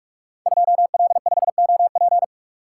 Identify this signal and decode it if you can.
Morse code, heard as a keyed tone, about 44 wpm, 700 Hz. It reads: 2L5OP
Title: Code Groups - 5 Letter:Numbers - 45wpm